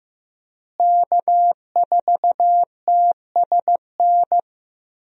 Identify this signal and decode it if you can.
Morse code, heard as a keyed tone, about 15 wpm, 700 Hz. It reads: K4TSN
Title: USA Call Signs - 5 Characters - Repeated 2x - 15wpm